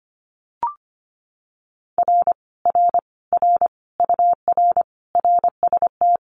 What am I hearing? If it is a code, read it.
Morse code, heard as a keyed tone, about 25 wpm, 700 Hz. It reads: R R R UR RST
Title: Common QSO Phrases Encore 25wpm